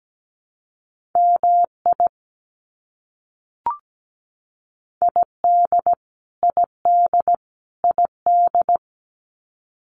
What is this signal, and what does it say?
Morse code, heard as a keyed tone, about 17 wpm, 700 Hz. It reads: MI ID ID ID
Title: US States Repeated 3x 17wpm